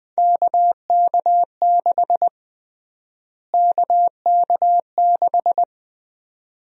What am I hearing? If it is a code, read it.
Morse code, heard as a keyed tone, about 20 wpm, 700 Hz. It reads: KK6 KK6
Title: USA Call Sign Prefixes - Repeated 2x - 20wpm